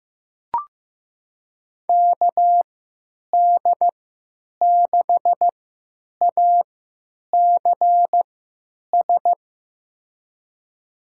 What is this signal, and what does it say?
Morse code, heard as a keyed tone, about 15 wpm, spaced as for 9 wpm, 700 Hz. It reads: KD6ACS
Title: USA Call Signs - 6 Characters - 3x Character Spacing - 15wpm